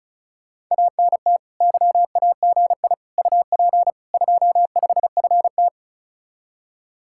Morse code, heard as a keyed tone, about 44 wpm, 700 Hz. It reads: ANT YAGI UP 25FT